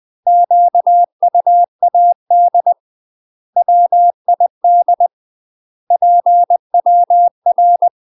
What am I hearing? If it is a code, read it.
Morse code, heard as a keyed tone, about 20 wpm, 700 Hz. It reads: QUAD WID PWR
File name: UK MASTER QSOs TRAINING/20WPM